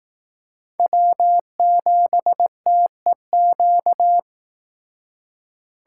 Morse code, heard as a keyed tone, about 18 wpm, 700 Hz. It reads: W7TEQ
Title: Speed Racing - USA 1x3 Call Signs 15wpm